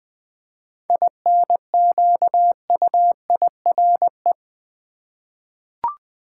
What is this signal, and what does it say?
Morse code, heard as a keyed tone, about 20 wpm, 700 Hz. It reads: INQUIRE
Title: 7 Letter Words - First 3 Letters Given - 20wpm